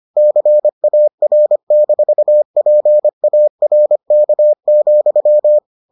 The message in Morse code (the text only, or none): CAR-PARK,